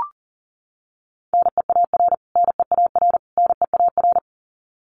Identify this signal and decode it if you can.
Morse code, heard as a keyed tone, about 40 wpm, 700 Hz. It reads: NEAR NEAR NEAR
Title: Top 500 Words Repeated 40wpm